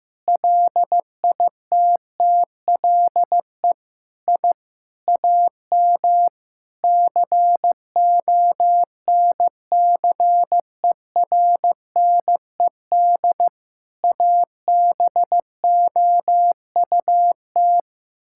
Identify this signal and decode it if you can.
Morse code, heard as a keyed tone, about 15 wpm, 700 Hz. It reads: LITTLE I AM CONCERNED ABOUT
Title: Sentences from Top 1000 Words - Review Part 6 - 15wpm